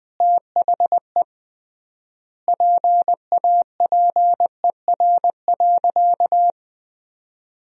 Morse code, heard as a keyed tone, about 21 wpm, 700 Hz. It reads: THE PAPER.
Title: 3x Word-Spacing - Sentences from Top 300 Words - 20wpm